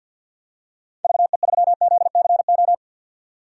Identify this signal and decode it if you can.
Morse code, heard as a keyed tone, about 52 wpm, 700 Hz. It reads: VE3ZCO
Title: DXCC-100 Call Signs - 6 Characters - 50wpm